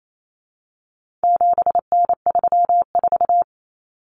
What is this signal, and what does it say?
Morse code, heard as a keyed tone, about 28 wpm, 700 Hz. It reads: 7N34
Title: Common Cut Numbers - 4-Digit 28wpm